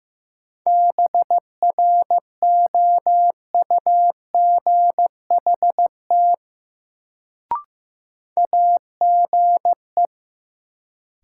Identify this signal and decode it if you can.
Morse code, heard as a keyed tone, about 15 wpm, 700 Hz. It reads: BROUGHT AGE
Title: Sentences from Top 500 Words - New Word Drill - 15wpm